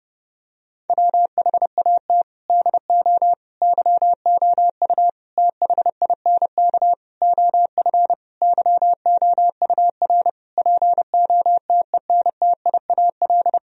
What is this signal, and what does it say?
Morse code, heard as a keyed tone, about 28 wpm, 700 Hz. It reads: WHAT DO YOU THINK OF YOUR POTENTIAL
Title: Sentences from Top 1000 Words - Review Part 4 - 30wpm